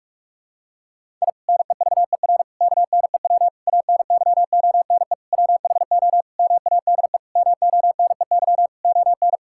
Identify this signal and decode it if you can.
Morse code, heard as a keyed tone, about 41 wpm, 700 Hz. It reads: I NEVER KNEW ANYONE WHO MADE MONEY ON
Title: Sentences from Top 1000 Words - Review Part 6 - 45wpm